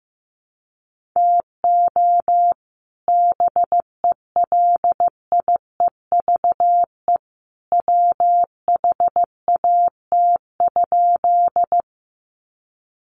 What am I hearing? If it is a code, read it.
Morse code, heard as a keyed tone, about 15 wpm, 700 Hz. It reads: TO BELIEVE WHAT?
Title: Sentences from Top 1000 Words 15wpm